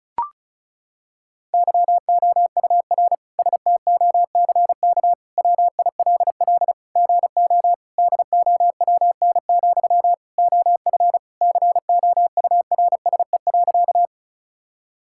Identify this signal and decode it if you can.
Morse code, heard as a keyed tone, about 36 wpm, 700 Hz. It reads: YOUR STOCK WILL GO DOWN, OF COURSE.
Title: Sentences from Top 1000 Words - Review Part 5 - 35wpm